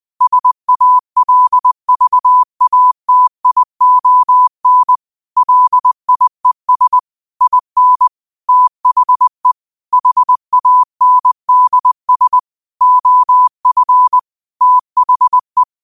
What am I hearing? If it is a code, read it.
Morse code, heard as a keyed tone, about 20 wpm, 1000 Hz. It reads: SALVATION LIES IN THE HANDS OF THE
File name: qotdfiles / 20wpm